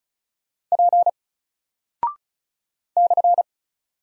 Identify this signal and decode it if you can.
Morse code, heard as a keyed tone, about 36 wpm, 700 Hz. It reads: P /
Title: Single Character (Letter:Number:Punctuation:Prosign) 35wpm